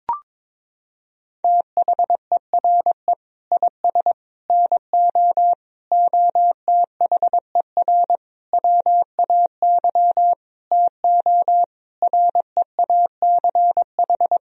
Morse code, heard as a keyed tone, about 22 wpm, 700 Hz. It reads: THERE IS NO OTHER WAY TO REACH
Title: Sentences from Top 1000 Words - Review Part 2 - 22wpm